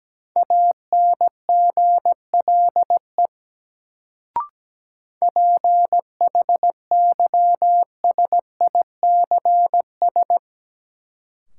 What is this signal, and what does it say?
Morse code, heard as a keyed tone, about 17 wpm, 700 Hz. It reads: ANGLE PHYSICS
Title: Sentences from Top 1500 Words - New Word Drill Alt - 17wpm